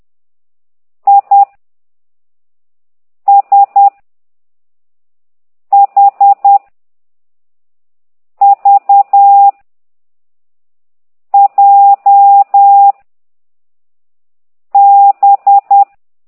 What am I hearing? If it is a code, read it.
Morse code, heard as a keyed tone, about 10 wpm, 800 Hz. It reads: I S H V J B
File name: ct2idw / exerc6